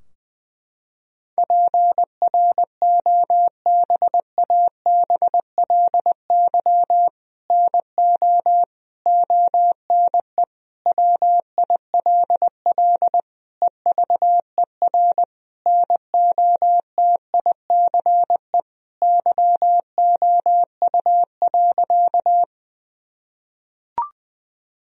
Morse code, heard as a keyed tone, about 20 wpm, 700 Hz. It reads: PROBABLY NO ONE WILL EVER NOTICE YOU.
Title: Sentences from Top 1000 Words - Review Part 4 - 20wpm